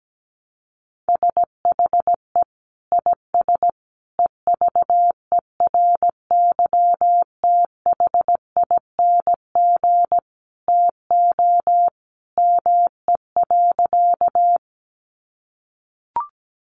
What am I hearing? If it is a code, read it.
Morse code, heard as a keyed tone, about 17 wpm, 700 Hz. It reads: SHE IS EVERYTHING TO ME.
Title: Sentences from Top 700 Words 17wpm